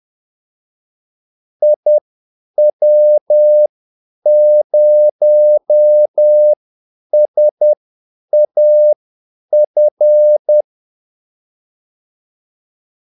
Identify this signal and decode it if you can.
Morse code, heard as a keyed tone, about 10 wpm, spaced as for 8 wpm, 600 Hz. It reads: IW0SAF
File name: Callsigns-10-8wpm